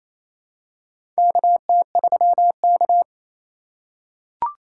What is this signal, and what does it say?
Morse code, heard as a keyed tone, about 28 wpm, 700 Hz. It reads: KT3K
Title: USA Call Signs - 4 Characters - 4x Character Spacing - 28wpm